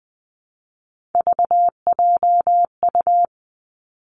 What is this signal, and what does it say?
Morse code, heard as a keyed tone, about 20 wpm, 700 Hz. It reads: VJU
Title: Three Challenging Letter-Number Combinations 20wpm